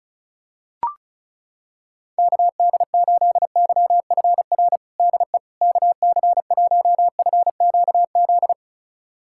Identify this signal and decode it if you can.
Morse code, heard as a keyed tone, about 36 wpm, 700 Hz. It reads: KD8YFR DE KC1FQZ
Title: USA Callsign DE Callsign 35wpm